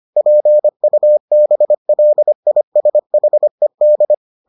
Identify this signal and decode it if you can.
Morse code, heard as a keyed tone, about 25 wpm, 600 Hz. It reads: PUBLISHED